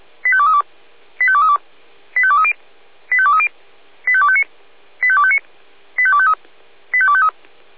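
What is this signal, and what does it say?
Five-tone selective call (ZVEI), heard as quick runs of five tones, tones of about 80 ms. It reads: 85232 85232 85230 85230 85259 85259 85253 85253
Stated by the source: ZVEI five-tone call 85852 received by SDR.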